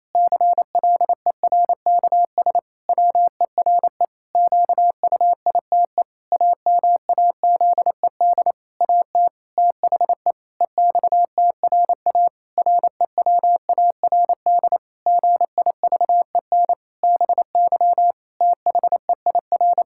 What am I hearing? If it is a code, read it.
Morse code, heard as a keyed tone, about 28 wpm, 700 Hz. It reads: CLERKS WERE QUITE AMAZED AT THE EXTRA REWARD GIVEN BY THEIR
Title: Pangram Challenge v2 - 28wpm